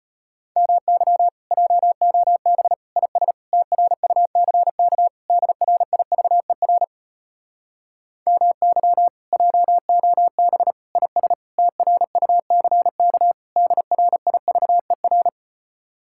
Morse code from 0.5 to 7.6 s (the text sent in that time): MY JOB IS TRUCK DRIVER